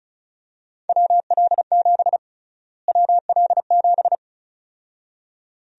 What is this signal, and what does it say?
Morse code, heard as a keyed tone, about 36 wpm, 700 Hz. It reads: WL7 WL7
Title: USA Call Sign Prefixes - Repeated 2x - 35wpm